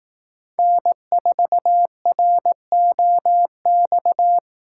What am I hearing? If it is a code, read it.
Morse code, heard as a keyed tone, about 18 wpm, 700 Hz. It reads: N4ROX
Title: Speed Racing - USA 1x3 Call Signs 15wpm